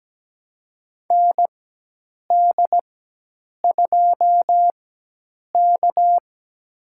Morse code, heard as a keyed tone, about 17 wpm, spaced as for 8 wpm, 700 Hz. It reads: ND2K
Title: USA Call Signs - 4 Characters - 4x Character Spacing - 17wpm